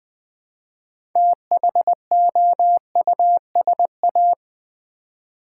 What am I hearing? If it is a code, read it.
Morse code, heard as a keyed tone, about 20 wpm, 700 Hz. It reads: THOUSA
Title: Build A Word - 2 Letters At A Time - 8-Letter Words - 20wpm